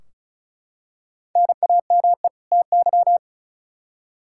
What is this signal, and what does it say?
Morse code, heard as a keyed tone, about 34 wpm, 700 Hz. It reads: NAME TY
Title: Practice QSO - Exchange 1 - Beginning - 35wpm